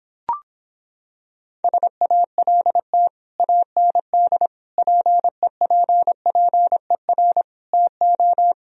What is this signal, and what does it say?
Morse code, heard as a keyed tone, about 26 wpm, 700 Hz. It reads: SALT AND PEPPER TO